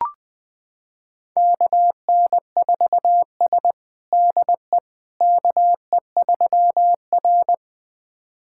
Morse code, heard as a keyed tone, about 21 wpm, 700 Hz. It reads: KN4S DE KE3R
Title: USA Callsign DE Callsign - 4 Characters - 20wpm